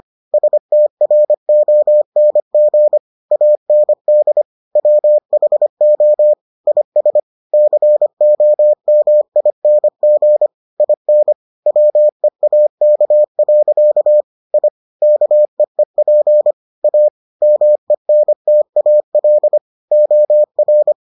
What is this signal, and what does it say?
Morse code, heard as a keyed tone, about 25 wpm, 600 Hz. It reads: STRONG AND WHO IS COMING IN WEAK. I KEEP A MENTAL OR